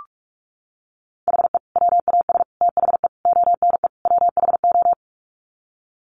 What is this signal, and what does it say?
Morse code, heard as a keyed tone, about 44 wpm, 700 Hz. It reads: HE WAS THE ONE WHO
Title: Sets of 5 Words Encore2 45wpm